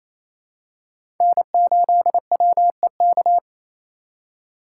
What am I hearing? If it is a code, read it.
Morse code, heard as a keyed tone, about 28 wpm, 700 Hz. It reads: N8WEK
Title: USA Call Signs - 3x Character Spacing - 28wpm